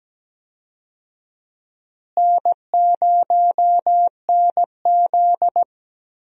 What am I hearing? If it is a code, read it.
Morse code, heard as a keyed tone, about 17 wpm, 700 Hz. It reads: N0NZ
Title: USA Call Signs - 4 Characters - Repeated 2x - 17wpm